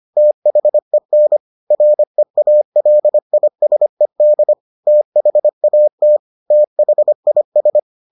Morse code, heard as a keyed tone, about 25 wpm, 600 Hz. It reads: THEN REALISED THAT THIS